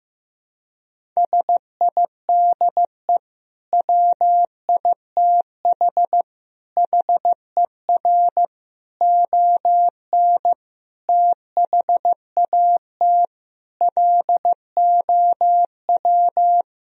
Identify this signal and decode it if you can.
Morse code, heard as a keyed tone, about 15 wpm, 700 Hz. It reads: SIDE WITH HER ON THAT LOW
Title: Sentences from Top 1000 Words - Review Part 1 - 15wpm